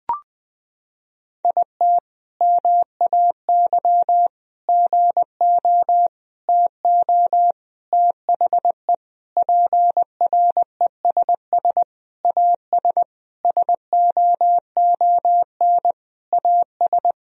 Morse code, heard as a keyed tone, about 20 wpm, 700 Hz. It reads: IT MAY GO TO THE PRESS AS SOON AS
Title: Sentences from Top 1000 Words - Review Part 1 - 20wpm